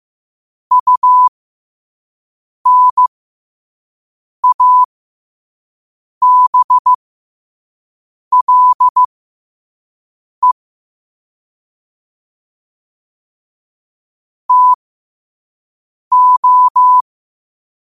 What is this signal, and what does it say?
Morse code, heard as a keyed tone, about 15 wpm, spaced as for 5 wpm, 1000 Hz. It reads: UNABLE TO
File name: qotdfiles / 5wpm